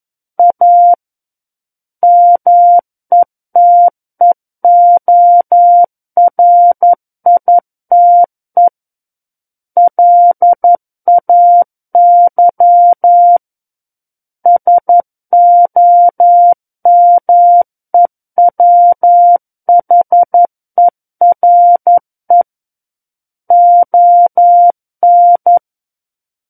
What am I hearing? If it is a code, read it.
Morse code, heard as a keyed tone, about 11 wpm, 700 Hz. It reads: A METEORITE LAY SOMEWHERE ON
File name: 11 WPM